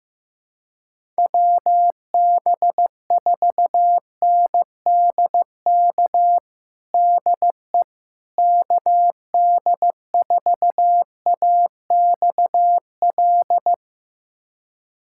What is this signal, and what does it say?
Morse code, heard as a keyed tone, about 15 wpm, 700 Hz. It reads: WB4NDK DE KD4AXL
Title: USA Callsign DE Callsign - 6 Characters - 15wpm